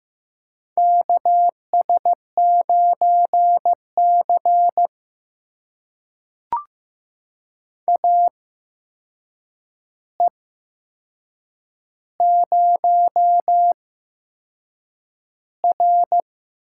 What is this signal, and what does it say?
Morse code, heard as a keyed tone, about 15 wpm, spaced as for 6 wpm, 700 Hz. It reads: KS9C AE0R
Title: USA Call Signs - 4 Characters - 8x Character Spacing - 15wpm